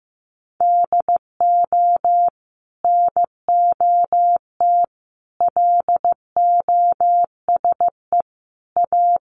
Morse code, heard as a keyed tone, about 15 wpm, 700 Hz. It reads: DO NOT LOSE A
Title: Sentences from Top 1100 Words 15wpm